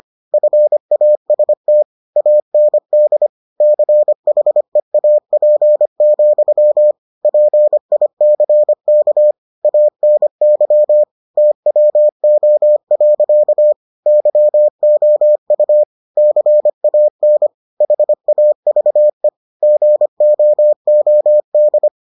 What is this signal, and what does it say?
Morse code, heard as a keyed tone, about 25 wpm, 600 Hz. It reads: FAST AND CHEAP, PICK ANY TWO. YOU CAN HAVE GOOD